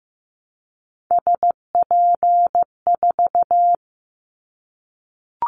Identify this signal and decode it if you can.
Morse code, heard as a keyed tone, about 15 wpm, 700 Hz. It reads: SP4
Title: Callsign Prefixes for Top 25 DXCC - Repeated 3x - 15wpm